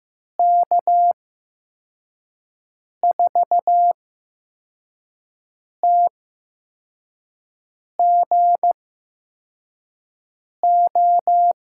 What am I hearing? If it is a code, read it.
Morse code, heard as a keyed tone, about 15 wpm, spaced as for 4 wpm, 700 Hz. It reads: K4TGO
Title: USA Call Signs - 6 Characters - 8x Character Spacing - 15wpm